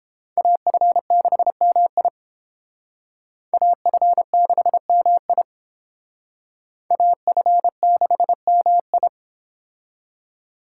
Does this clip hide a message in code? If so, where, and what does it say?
Yes: Morse code from 0.4 to 9.1 s, heard as a keyed tone, about 30 wpm, 700 Hz. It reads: AF6MI AF6MI AF6MI